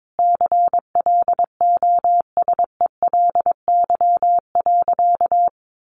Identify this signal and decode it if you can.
Morse code, heard as a keyed tone, about 22 wpm, 700 Hz. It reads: CLOSELY.
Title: Sentences from Top 1400 Words 22wpm